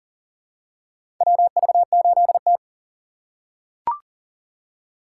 Morse code, heard as a keyed tone, about 40 wpm, 700 Hz. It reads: WV8T
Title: USA Call Signs - 4 Characters - 4x Character Spacing - 40wpm